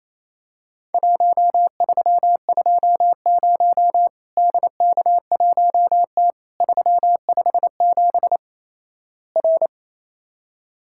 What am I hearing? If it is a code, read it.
Morse code, heard as a keyed tone, about 28 wpm, 700 Hz. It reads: 1320 DK1T 357 R
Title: Worked All Europe DX Contest - QTC Exchange - 28wpm